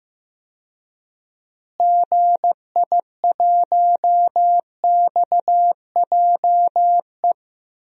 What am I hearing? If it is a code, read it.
Morse code, heard as a keyed tone, about 16 wpm, 700 Hz. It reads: GI1XJE
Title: DXCC-100 Call Signs - 6 Characters - Repeated 2x - 15wpm